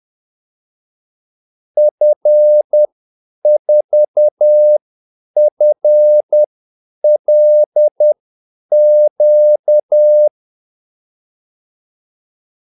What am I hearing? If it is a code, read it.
Morse code, heard as a keyed tone, about 10 wpm, spaced as for 8 wpm, 600 Hz. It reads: F4FLQ
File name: Callsigns-10-8wpm